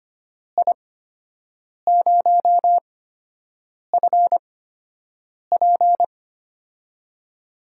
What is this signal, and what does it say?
Morse code, heard as a keyed tone, about 25 wpm, spaced as for 7 wpm, 700 Hz. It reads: I0FP
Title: USA Call Signs - 5 Characters - 8x Character Spacing - 25wpm